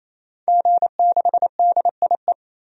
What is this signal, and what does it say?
Morse code, heard as a keyed tone, about 28 wpm, 700 Hz. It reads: G6DIE
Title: DXCC-100 Call Signs - 5 Characters - 28wpm